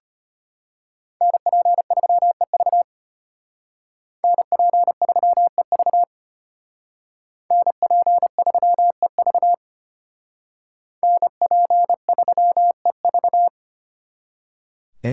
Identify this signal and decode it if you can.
Morse code, heard as a keyed tone, about 33 wpm, 700 Hz. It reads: NP3EV NP3EV NP3EV NP3EV